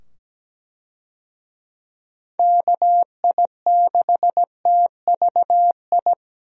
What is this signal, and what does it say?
Morse code, heard as a keyed tone, about 17 wpm, 700 Hz. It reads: KI6TVI
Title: USA Call Signs - 6 Characters - 17wpm